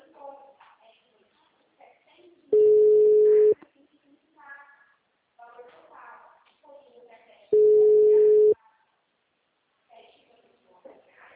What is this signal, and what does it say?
Ringback tone (Europe, 425 Hz), with speech.